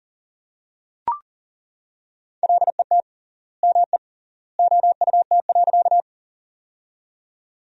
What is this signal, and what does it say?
Morse code, heard as a keyed tone, about 40 wpm, 700 Hz. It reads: LET ME OUT.
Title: 3x Word-Spacing - Sentences from Top 300 Words - 40wpm